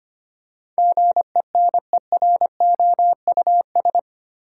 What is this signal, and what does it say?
Morse code, heard as a keyed tone, about 25 wpm, 690 Hz. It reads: GENEROUS
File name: 8 Letter Words - First 3 Letters Given - 25wpm